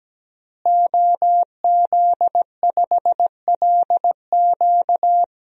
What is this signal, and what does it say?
Morse code, heard as a keyed tone, about 17 wpm, 700 Hz. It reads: OZ5LQ
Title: DXCC-100 Call Signs - 5 Characters - Repeated 3x - 17wpm